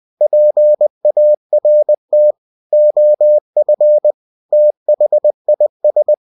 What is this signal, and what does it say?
Morse code, heard as a keyed tone, about 20 wpm, 600 Hz. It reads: PART OF THIS